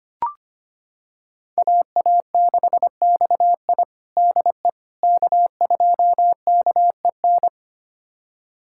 Morse code, heard as a keyed tone, about 25 wpm, 700 Hz. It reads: AA6XI DE K2KEN
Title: USA Callsign DE Callsign - 5 Characters - 25wpm